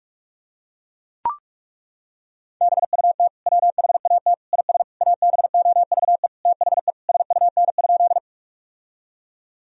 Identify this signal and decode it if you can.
Morse code, heard as a keyed tone, about 41 wpm, 700 Hz. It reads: BUT WHAT IS ABOVE THE SUN?
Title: Sentences from Top 1000 Words - Review Part 2 - 45wpm